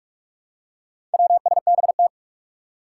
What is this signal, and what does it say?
Morse code, heard as a keyed tone, about 44 wpm, 700 Hz. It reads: WSBT
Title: Four Challenging Letter-Number Combinations 45wpm